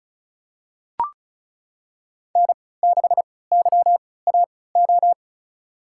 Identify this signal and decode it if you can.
Morse code, heard as a keyed tone, about 36 wpm, spaced as for 20 wpm, 690 Hz. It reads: N6YAO
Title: USA Call Signs - 5 Characters - 3x Character Spacing - 35wpm